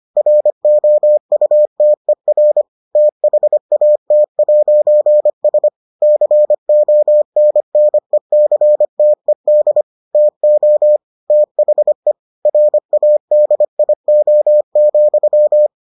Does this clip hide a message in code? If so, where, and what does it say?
Yes: Morse code from 0.2 to 15.7 s, heard as a keyed tone, about 25 wpm, 600 Hz. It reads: ROUTER THAT'S CONNECTED TO THE RADIO,